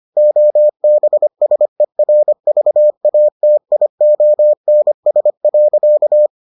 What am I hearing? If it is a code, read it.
Morse code, heard as a keyed tone, about 25 wpm, 600 Hz. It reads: OBSERVATIONS.